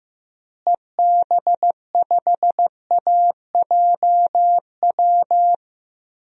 Morse code, heard as a keyed tone, about 15 wpm, 700 Hz. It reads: EB5AJW
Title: DXCC-100 Call Signs - 6 Characters - Repeated 3x - 15wpm